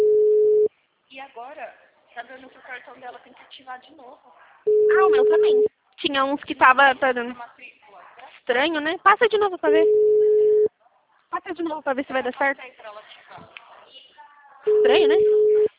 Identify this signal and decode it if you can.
Ringback tone (Europe, 425 Hz), with speech.